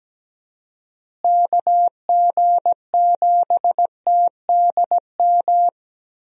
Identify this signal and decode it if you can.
Morse code, heard as a keyed tone, about 17 wpm, 700 Hz. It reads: KG7TDM
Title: USA Call Signs - 6 Characters - Repeated 2x - 17wpm